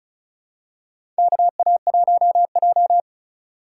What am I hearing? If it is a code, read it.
Morse code, heard as a keyed tone, about 36 wpm, 700 Hz. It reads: KA1J
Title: USA Call Signs - 3x Character Spacing - 35wpm